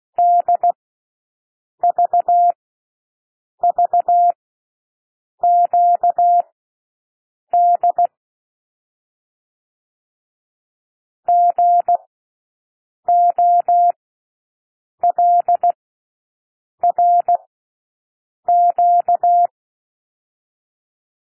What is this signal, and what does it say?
Morse code, heard as a keyed tone, about 16 wpm, spaced as for 6 wpm, 700 Hz. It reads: DVVQD GOLRQ
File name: cwmp3 / L1 SOUND LETTERS 16 wpm